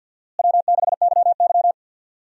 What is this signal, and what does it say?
Morse code, heard as a keyed tone, about 53 wpm, 700 Hz. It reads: W6YY
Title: USA Call Signs - 4 Characters - Repeated 2x - 50wpm